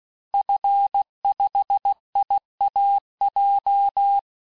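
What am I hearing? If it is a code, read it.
Morse code, heard as a keyed tone, about 16 wpm, 800 Hz. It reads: F5IAJ
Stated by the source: Morse course text, end of QSO, sent at 15 words per minute.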